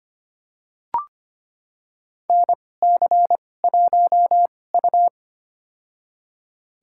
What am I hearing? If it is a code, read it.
Morse code, heard as a keyed tone, about 25 wpm, spaced as for 18 wpm, 700 Hz. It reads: NC1U
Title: USA Call Signs - 4 Characters - 2x Character Spacing - 25wpm